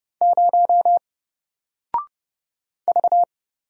Morse code, heard as a keyed tone, about 30 wpm, 700 Hz. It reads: 0 V
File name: Single-Letter-Number-30wpm